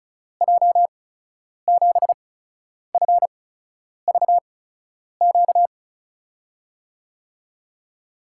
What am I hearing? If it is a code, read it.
Morse code, heard as a keyed tone, about 36 wpm, spaced as for 10 wpm, 700 Hz. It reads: J7FVQ
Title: USA Call Signs - 6 Characters - 8x Character Spacing - 35wpm